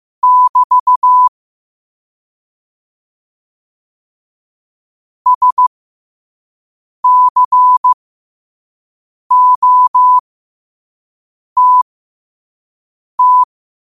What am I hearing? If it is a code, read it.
Morse code, heard as a keyed tone, about 15 wpm, spaced as for 5 wpm, 1000 Hz. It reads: <BT> SCOTT
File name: qotdfiles / 5wpm